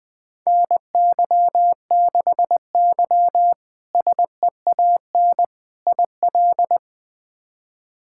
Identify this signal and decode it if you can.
Morse code, heard as a keyed tone, about 21 wpm, 700 Hz. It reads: NY6Y SEAN IL
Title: K1USN SST QSO Practice - 20wpm